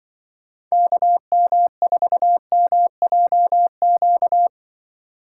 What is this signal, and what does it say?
Morse code, heard as a keyed tone, about 24 wpm, 700 Hz. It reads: KM4MJQ
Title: Speed Racing - USA 2x3 Call Signs 20wpm